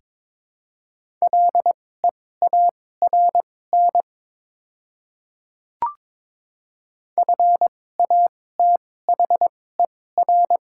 Morse code, heard as a keyed tone, about 22 wpm, spaced as for 16 wpm, 700 Hz. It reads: LEARN FATHER
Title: 2x Character Spacing - Top 300 Words - 22wpm